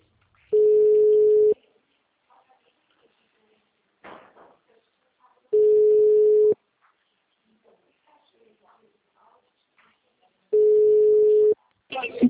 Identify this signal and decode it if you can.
Ringback tone (Europe, 425 Hz), with speech.